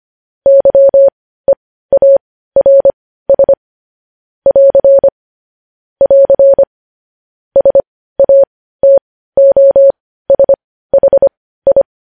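Morse code, heard as a keyed tone, about 25 wpm, spaced as for 15 wpm, 550 Hz. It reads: YEARS <AR> <AR> SATOSHI
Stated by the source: Morse news headlines sent at 15 wpm.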